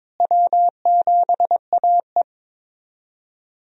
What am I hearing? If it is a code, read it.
Morse code, heard as a keyed tone, about 22 wpm, 700 Hz. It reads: W7AE
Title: USA Call Signs - 4 Characters - Repeated 2x - 22wpm